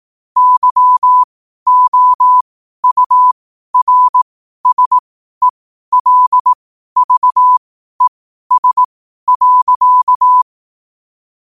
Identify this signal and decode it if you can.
Morse code, heard as a keyed tone, about 18 wpm, spaced as for 12 wpm, 1000 Hz. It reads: YOURSELVES.
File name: qotdfiles / 12wpm